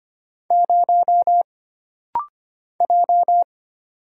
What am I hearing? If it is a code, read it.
Morse code, heard as a keyed tone, about 25 wpm, 700 Hz. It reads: 0 J
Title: Single Letter-Number - Rapid-Fire 25wpm